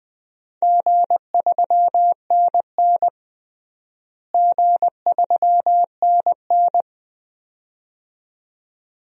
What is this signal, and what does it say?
Morse code, heard as a keyed tone, about 21 wpm, 700 Hz. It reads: G3NN G3NN
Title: DXCC-100 Call Signs - 4 Characters - Repeated 2x - 20wpm